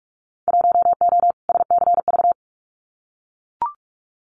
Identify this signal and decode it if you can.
Morse code, heard as a keyed tone, about 44 wpm, 700 Hz. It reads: 1O SCV
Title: Winter Field Day Exchanges 45wpm